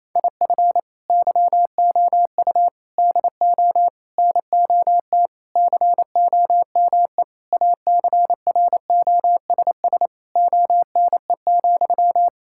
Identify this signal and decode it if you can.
Morse code, heard as a keyed tone, about 28 wpm, 700 Hz. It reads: IF YOU DO NOT COME ACROSS ONE,
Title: Sentences from Top 1000 Words - Review Part 5 - 28wpm